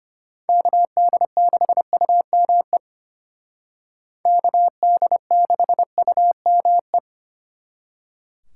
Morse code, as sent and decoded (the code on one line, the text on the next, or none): -.- -.. -.... ..- -- . / -.- -.. -.... ..- -- .
KD6UME KD6UME